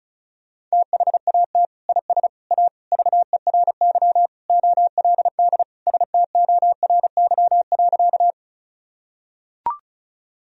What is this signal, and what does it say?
Morse code, heard as a keyed tone, about 36 wpm, 700 Hz. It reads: THAT IS A VERY OLD STORY.
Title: Sentences from Top 1000 Words - Review Part 1 - 35wpm